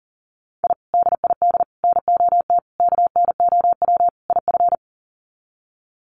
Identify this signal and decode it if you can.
Morse code, heard as a keyed tone, about 38 wpm, 700 Hz. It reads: I DID NOT KNOW IF
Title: Sets of 5 Words Encore 40wpm